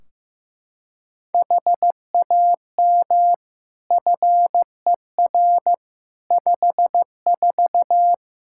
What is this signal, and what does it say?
Morse code, heard as a keyed tone, about 15 wpm, 700 Hz. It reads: HAM FER 54
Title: Practice QSO - Exchange 3 - More Info - 15wpm